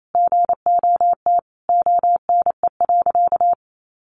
Morse code, heard as a keyed tone, about 28 wpm, 700 Hz. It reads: GOT ONE.
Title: Sentences from Top 300 Words 28wpm